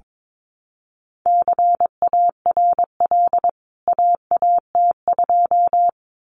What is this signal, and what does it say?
Morse code, heard as a keyed tone, about 22 wpm, 700 Hz. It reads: CARL AAT2
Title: CWT Exchange 22wpm